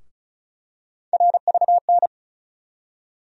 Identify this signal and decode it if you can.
Morse code, heard as a keyed tone, about 36 wpm, 700 Hz. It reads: RVN
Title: USA Call Sign Postfixes - 3 Letters 35wpm